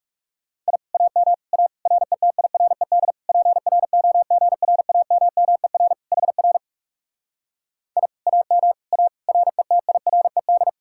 Morse code, heard as a keyed tone, about 40 wpm, 700 Hz. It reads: I AM A RETIRED PROGRAMMER HR I AM A RETIRED